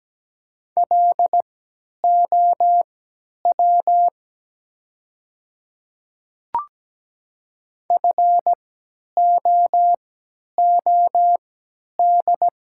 Morse code, heard as a keyed tone, about 17 wpm, spaced as for 10 wpm, 700 Hz. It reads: LOW FOOD
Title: 3x Character Spacing - Top 300 Words - 17wpm